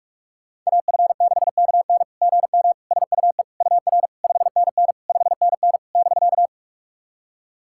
Morse code, heard as a keyed tone, about 46 wpm, 700 Hz. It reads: AF6KN GM SUE UR 5NN 5NN <BK>
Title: SOTA QSO Practice 45wpm